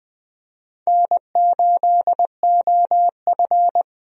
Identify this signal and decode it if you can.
Morse code, heard as a keyed tone, about 20 wpm, 700 Hz. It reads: N8OF
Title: USA Call Signs - 4 Characters - Repeated 3x - 20wpm